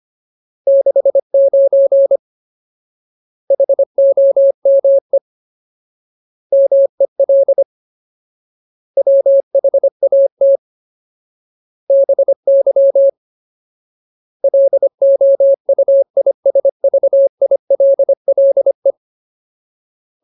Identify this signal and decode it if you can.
Morse code, heard as a keyed tone, about 25 wpm, 550 Hz. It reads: 69 HOME MEL WHAT BY LOUISVILLE
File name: Ultimate Word List / 25WPM Ultimate